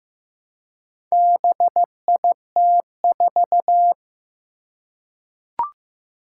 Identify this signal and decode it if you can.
Morse code, heard as a keyed tone, about 15 wpm, 700 Hz. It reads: BIT4
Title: Four Challenging Letter-Number Combinations 15wpm